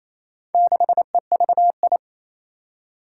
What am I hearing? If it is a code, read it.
Morse code, heard as a keyed tone, about 28 wpm, 700 Hz. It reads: 6EVI
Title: Four Challenging Letter-Number Combinations 28wpm